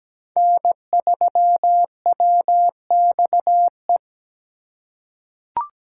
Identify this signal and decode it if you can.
Morse code, heard as a keyed tone, about 17 wpm, 700 Hz. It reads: N3WXE
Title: USA Call Signs - 5 Characters - 3x Character Spacing - 17wpm